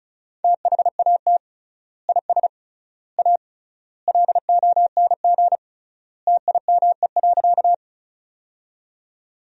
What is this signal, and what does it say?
Morse code, heard as a keyed tone, about 36 wpm, 700 Hz. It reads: THAT IS A LONG TIME.
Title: 3x Word-Spacing - Sentences from Top 100 Words - 35wpm